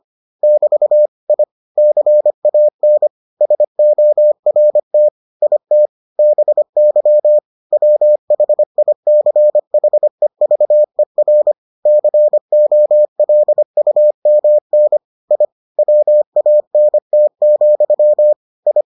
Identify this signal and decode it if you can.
Morse code, heard as a keyed tone, about 25 wpm, 600 Hz. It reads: <BT> I CAN SORT IT BY WHICHEVER COLUMN I WANT, I